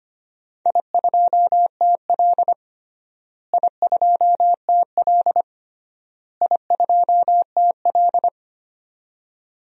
Morse code, heard as a keyed tone, about 25 wpm, 700 Hz. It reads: I2TL I2TL I2TL
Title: DXCC-100 Call Signs - 4 Characters - Repeated 3x - 25wpm